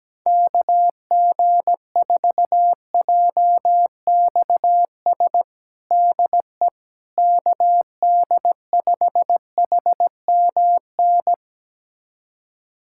Morse code, heard as a keyed tone, about 17 wpm, 700 Hz. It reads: KG4JXS DE KD5HMN
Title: USA Callsign DE Callsign - 6 Characters - 17wpm